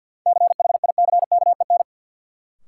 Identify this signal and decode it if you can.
Morse code, heard as a keyed tone, about 52 wpm, 700 Hz. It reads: CHICKEN
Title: Sentences from Top 1000 Words - Review Part 5 - New Word Drill - 50wpm